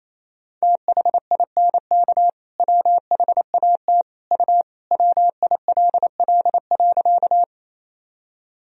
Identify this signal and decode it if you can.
Morse code, heard as a keyed tone, about 28 wpm, 700 Hz. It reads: THINK WHAT U WILL.
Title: Sentences from Top 200 Words and Abbreviations 28wpm